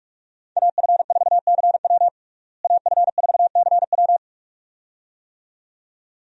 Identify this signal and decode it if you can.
Morse code, heard as a keyed tone, about 42 wpm, 700 Hz. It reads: AF4CW AF4CW
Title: USA Call Signs - 5 Characters - Repeated 2x - 45wpm